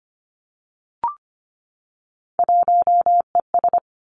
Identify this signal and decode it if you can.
Morse code, heard as a keyed tone, about 25 wpm, 700 Hz. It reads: 1ES
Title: Three Challenging Letter-Number Combinations 25wpm